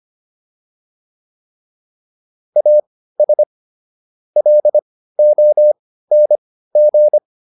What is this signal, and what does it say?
Morse code, heard as a keyed tone, about 25 wpm, spaced as for 15 wpm, 600 Hz. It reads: AS LONG